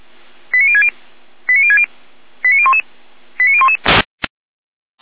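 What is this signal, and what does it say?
Five-tone selective call (ZVEI), heard as quick runs of five tones, tones of about 70 ms. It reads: 89060 89060 89011 89011
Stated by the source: ZVEI five-tone call 89017 received by SDR, on an audio server of the Feuerwehr Orlamünde fire brigade.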